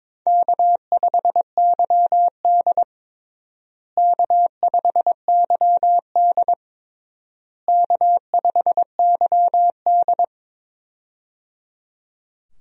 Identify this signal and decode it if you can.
Morse code, heard as a keyed tone, about 22 wpm, 700 Hz. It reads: K5YD K5YD K5YD
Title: USA Call Signs - 4 Characters - Repeated 3x - 22wpm